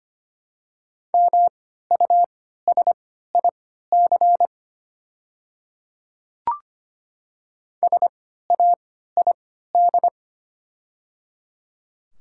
Morse code, heard as a keyed tone, about 25 wpm, spaced as for 14 wpm, 700 Hz. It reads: MUSIC SAID
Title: 3x Character Spacing - Top 500 Words - 25wpm